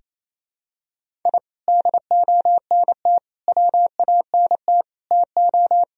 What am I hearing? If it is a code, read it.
Morse code, heard as a keyed tone, about 28 wpm, 700 Hz. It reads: I DONT WANT TO
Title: Sentences from Top 1000 Words - Review Part 1 - 28wpm